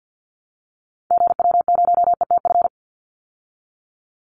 Morse code, heard as a keyed tone, about 50 wpm, 700 Hz. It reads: GW0ETF
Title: Call Signs Repeated 2x - 50wpm